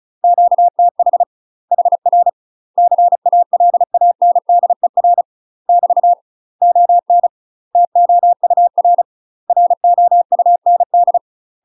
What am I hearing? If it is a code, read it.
Morse code, heard as a keyed tone, about 34 wpm, 700 Hz. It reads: QTH HR CALANDER <BT> ON TOUR ROUND
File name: UK MASTER QSOs TRAINING / 35WPM / UK MASTER QSOs 0037